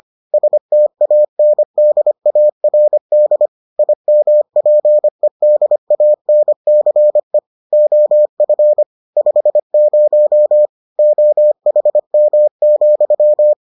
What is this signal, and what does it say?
Morse code, heard as a keyed tone, about 25 wpm, 600 Hz. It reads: STANDARD IMPEDANCE OF 50 OHM,